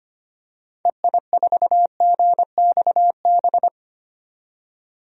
Morse code, heard as a keyed tone, about 25 wpm, 700 Hz. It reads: EI4GXB
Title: DXCC-100 Call Signs - 6 Characters - 25wpm